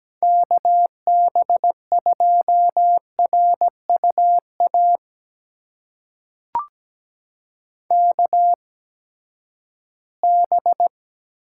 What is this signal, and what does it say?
Morse code, heard as a keyed tone, about 17 wpm, spaced as for 12 wpm, 700 Hz. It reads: KB2RUA KB
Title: USA Call Signs - 6 Characters - 8x Character Spacing - 17wpm